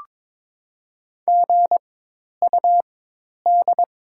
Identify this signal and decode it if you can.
Morse code, heard as a keyed tone, about 22 wpm, spaced as for 10 wpm, 700 Hz. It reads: GUD
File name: Top 20 QSO Elements - 4x Character Spacing - 22wpm